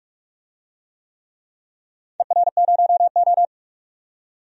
Morse code, heard as a keyed tone, about 42 wpm, 700 Hz. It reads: ER0O
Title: DXCC-100 Call Signs - 4 Characters - 45wpm